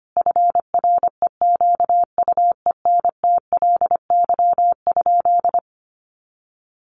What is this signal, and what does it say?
Morse code, heard as a keyed tone, about 25 wpm, 700 Hz. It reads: FREQUENTLY?
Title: Sentences from Top 1300 Words 25wpm